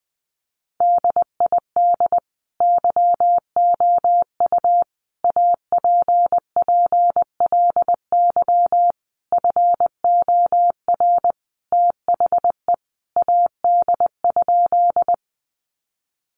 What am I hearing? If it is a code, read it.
Morse code, heard as a keyed tone, about 20 wpm, 700 Hz. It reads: DID YOU APPLY FOR THE AD?
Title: Sentences from Top 1300 Words 20wpm